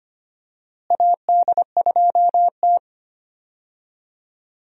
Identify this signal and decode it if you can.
Morse code, heard as a keyed tone, about 25 wpm, 700 Hz. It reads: AD2T
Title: USA Call Signs - 4 Characters - Repeated 3x - 25wpm